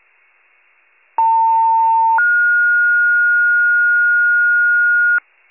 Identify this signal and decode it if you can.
Two-tone page, heard as a one-second tone, then a three-second tone, about 900 then 1430 Hz, with noise. It is heard from 1.2 to 5.2 s.
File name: pager / 78 FIRE / wcalert